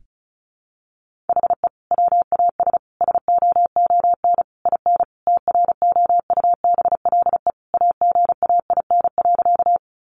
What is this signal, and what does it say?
Morse code, heard as a keyed tone, about 34 wpm, 700 Hz. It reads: HE WAS SOON IN TROUBLE AGAIN.
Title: Sentences from Top 1200 Words 35wpm